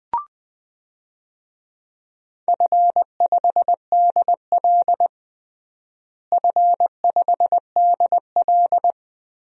Morse code, heard as a keyed tone, about 21 wpm, 700 Hz. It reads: F5DL F5DL
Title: DXCC-100 Call Signs - 4 Characters - Repeated 3x - 20wpm